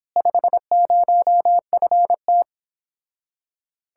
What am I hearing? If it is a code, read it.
Morse code, heard as a keyed tone, about 29 wpm, 700 Hz. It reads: 50FT